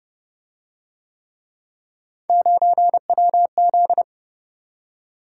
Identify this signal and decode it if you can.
Morse code, heard as a keyed tone, about 29 wpm, 700 Hz. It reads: 9WZ